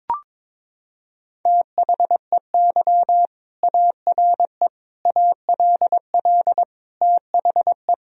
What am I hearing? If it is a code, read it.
Morse code, heard as a keyed tone, about 22 wpm, 700 Hz. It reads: THEY ARE ALL THE
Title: Sentences from Top 1000 Words - Review Part 4 - 22wpm